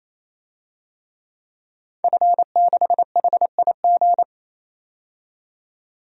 Morse code, heard as a keyed tone, about 28 wpm, 700 Hz. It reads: F6HIG
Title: DXCC-100 Call Signs - 5 Characters - 28wpm